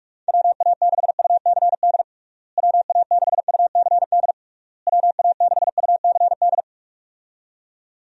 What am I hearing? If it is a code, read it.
Morse code, heard as a keyed tone, about 46 wpm, 700 Hz. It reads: WA6UCD WA6UCD WA6UCD
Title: USA Call Signs - 6 Characters - Repeated 3x - 45wpm